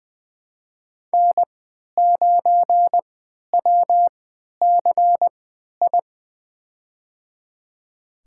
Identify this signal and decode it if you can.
Morse code, heard as a keyed tone, about 20 wpm, spaced as for 11 wpm, 700 Hz. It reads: N9WCI
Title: USA Call Signs - 5 Characters - 3x Character Spacing - 20wpm